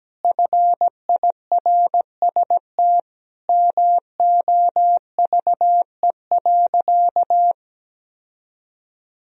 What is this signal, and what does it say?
Morse code, heard as a keyed tone, about 17 wpm, 700 Hz. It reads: FIRST MOVE.
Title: Sentences from Top 1000 Words - Review Part 1 - 17wpm